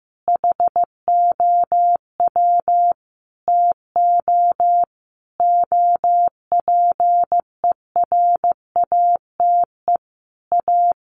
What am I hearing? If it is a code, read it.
Morse code, heard as a keyed tone, about 15 wpm, 700 Hz. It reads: HOW TO OPERATE A
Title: Sentences from Top 1500 Words 15wpm